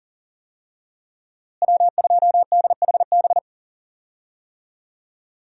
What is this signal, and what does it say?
Morse code, heard as a keyed tone, about 40 wpm, 700 Hz. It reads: W2DHB
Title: USA Call Signs - 5 Characters - Repeated 3x - 40wpm